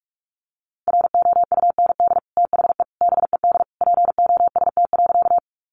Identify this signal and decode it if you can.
Morse code, heard as a keyed tone, about 44 wpm, 700 Hz. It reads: ROUND THE BED POST.
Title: Sentences from Top 1000 Words 45wpm